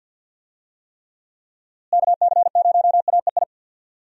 Morse code, heard as a keyed tone, about 49 wpm, 700 Hz. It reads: KC0AEI
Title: USA Call Signs - 6 Characters - Repeated 2x - 50wpm